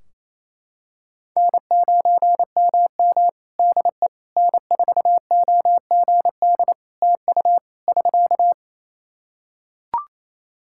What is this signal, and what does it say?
Morse code, heard as a keyed tone, about 27 wpm, 700 Hz. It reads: N9MM DE N4OGD TU <SK>
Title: K1USN SST QSO Practice 28wpm